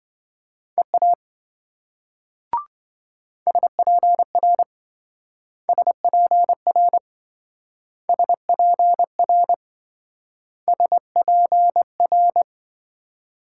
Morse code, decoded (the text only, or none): EA SPR SPR SPR SPR